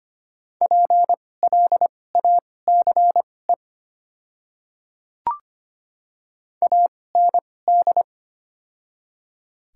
Morse code, heard as a keyed tone, about 25 wpm, spaced as for 18 wpm, 690 Hz. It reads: PLACE AND
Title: 2x Character Spacing - Top 300 Words - 25wpm